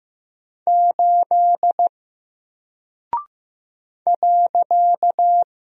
Morse code, heard as a keyed tone, about 15 wpm, 700 Hz. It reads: 8 .
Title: Single Character (Letter:Number:Punctuation:Prosign) 15wpm